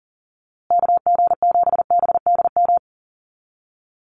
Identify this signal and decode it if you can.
Morse code, heard as a keyed tone, about 40 wpm, 700 Hz. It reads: KG7BDM
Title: USA Call Signs Repeated 2x 40wpm